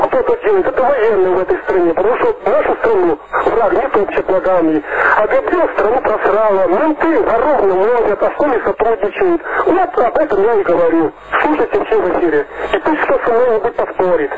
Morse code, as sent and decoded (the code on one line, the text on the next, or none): none